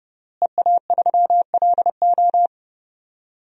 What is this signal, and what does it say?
Morse code, heard as a keyed tone, about 30 wpm, 700 Hz. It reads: EA3LO
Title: DXCC-100 Call Signs - 5 Characters - Repeated 3x - 30wpm